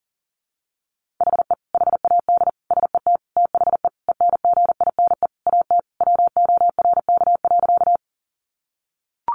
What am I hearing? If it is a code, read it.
Morse code, heard as a keyed tone, about 40 wpm, 700 Hz. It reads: HE HAD SET THE ENGINE AT WORK.
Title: Sentences from Top 1500 Words 40wpm